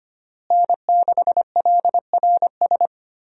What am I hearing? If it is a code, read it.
Morse code, heard as a keyed tone, about 25 wpm, 700 Hz. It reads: N6LRS
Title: USA Call Signs - 5 Characters - Repeated 3x - 25wpm